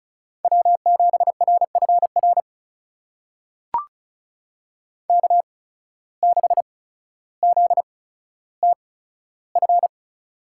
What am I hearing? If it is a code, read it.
Morse code, heard as a keyed tone, about 36 wpm, spaced as for 15 wpm, 700 Hz. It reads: W7RFR K6ZTF
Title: USA Call Signs - 5 Characters - 8x Character Spacing - 35wpm